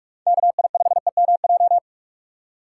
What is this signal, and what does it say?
Morse code, heard as a keyed tone, about 46 wpm, 700 Hz. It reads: KI5EMJ
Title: USA Call Signs - 4x Character Spacing - 45wpm